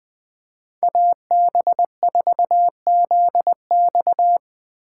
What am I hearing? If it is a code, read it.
Morse code, heard as a keyed tone, about 21 wpm, 700 Hz. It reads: AB4ZX
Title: USA Call Signs - 5 Characters - 4x Character Spacing - 20wpm